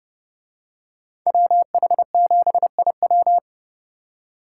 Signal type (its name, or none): Morse code